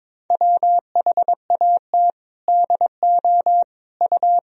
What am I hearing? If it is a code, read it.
Morse code, heard as a keyed tone, about 22 wpm, 700 Hz. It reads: WHAT DO U
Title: Sentences from Top 200 Words and Abbreviations 22wpm